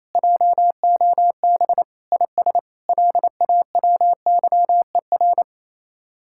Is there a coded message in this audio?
Yes: Morse code from 0.1 to 5.4 s, heard as a keyed tone, about 36 wpm, 700 Hz. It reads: JOB IS LAWYER